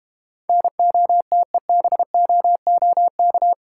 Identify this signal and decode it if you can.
Morse code, heard as a keyed tone, about 32 wpm, 700 Hz. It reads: NOTEBOOK